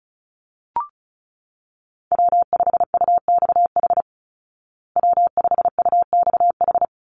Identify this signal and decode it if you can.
Morse code, heard as a keyed tone, about 34 wpm, 700 Hz. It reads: W5UXH W5UXH
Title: Call Signs Repeated 2x - 35wpm